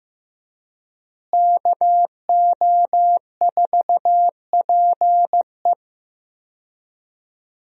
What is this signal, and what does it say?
Morse code, heard as a keyed tone, about 15 wpm, 700 Hz. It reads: KO4PE
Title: USA Call Signs - 5 Characters - Repeated 2x - 15wpm